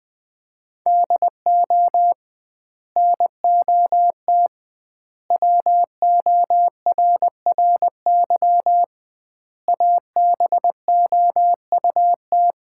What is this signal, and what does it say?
Morse code, heard as a keyed tone, about 21 wpm, 700 Hz. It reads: DO NOT WORRY ABOUT
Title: 2x Word-Spacing - Sets of 5 Words - 20wpm